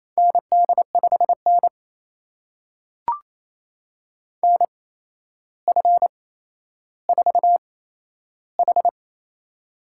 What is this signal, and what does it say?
Morse code, heard as a keyed tone, about 28 wpm, spaced as for 12 wpm, 700 Hz. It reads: ND5N NF4H
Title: USA Call Signs - 4 Characters - 8x Character Spacing - 28wpm